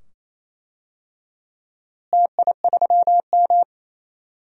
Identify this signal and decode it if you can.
Morse code, heard as a keyed tone, about 28 wpm, 700 Hz. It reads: TI3M
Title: DXCC-100 Call Signs - 4 Characters - Repeated 2x - 28wpm